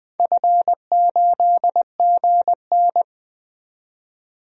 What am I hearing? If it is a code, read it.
Morse code, heard as a keyed tone, about 20 wpm, 700 Hz. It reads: F8GN
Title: DXCC-100 Call Signs - 4 Characters - 20wpm